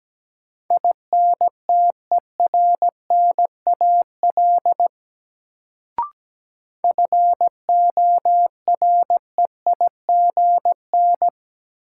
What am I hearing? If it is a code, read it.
Morse code, heard as a keyed tone, about 17 wpm, 700 Hz. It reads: INTERNAL FOREIGN
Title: Sentences from Top 1400 Words - New Word Drill Alt - 17wpm